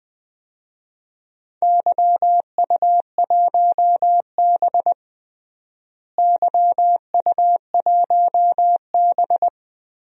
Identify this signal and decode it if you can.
Morse code, heard as a keyed tone, about 20 wpm, 700 Hz. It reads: YU1B YU1B
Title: DXCC-100 Call Signs - 4 Characters - Repeated 3x - 20wpm